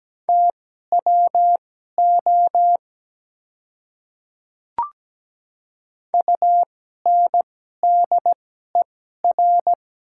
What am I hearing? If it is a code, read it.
Morse code, heard as a keyed tone, about 17 wpm, spaced as for 12 wpm, 700 Hz. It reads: TWO UNDER
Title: 2x Character Spacing - Top 300 Words - 17wpm